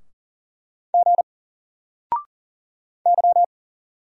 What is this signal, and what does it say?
Morse code, heard as a keyed tone, about 40 wpm, 700 Hz. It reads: G Y
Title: Single Letter-Number 40wpm